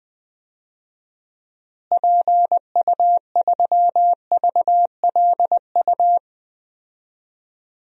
Morse code, heard as a keyed tone, about 20 wpm, 700 Hz. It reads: PU3VLU
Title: DXCC-100 Call Signs - 6 Characters - Repeated 2x - 20wpm